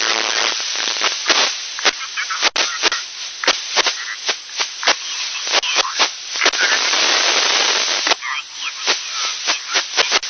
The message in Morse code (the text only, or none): none